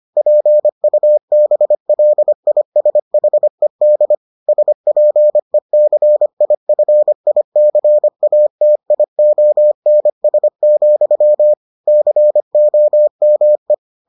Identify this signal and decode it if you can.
Morse code, heard as a keyed tone, about 25 wpm, 600 Hz. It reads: PUBLISHED SPECIFICATIONS, COME